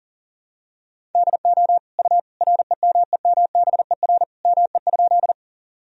Morse code, heard as a keyed tone, about 40 wpm, 700 Hz. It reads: DO U REMEMBER ME?
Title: Sentences from Top 400 Words and Abbreviations 40wpm